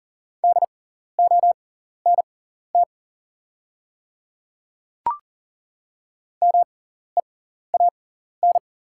Morse code, heard as a keyed tone, about 40 wpm, spaced as for 14 wpm, 690 Hz. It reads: DONT MEAN
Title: 6x Character Spacing - Top 300 Words - 40wpm